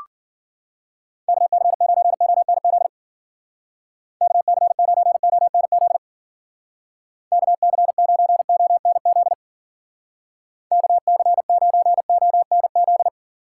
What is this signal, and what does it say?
Morse code, heard as a keyed tone, about 53 wpm, 700 Hz. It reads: KC9ONZ KC9ONZ KC9ONZ KC9ONZ